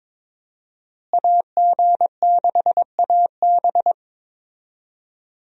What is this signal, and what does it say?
Morse code, heard as a keyed tone, about 22 wpm, 700 Hz. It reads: AG6AB
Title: USA Call Signs - 5 Characters - Repeated 3x - 22wpm